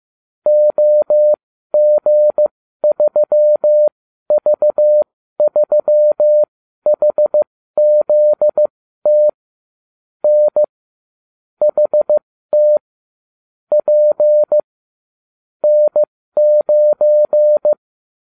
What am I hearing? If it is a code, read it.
Morse code, heard as a keyed tone, about 15 wpm, spaced as for 12 wpm, 600 Hz. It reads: OG3V3HZT N HT P N9